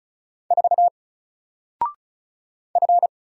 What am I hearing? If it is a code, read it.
Morse code, heard as a keyed tone, about 34 wpm, 700 Hz. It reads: 4 F
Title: Single Letter-Number 35wpm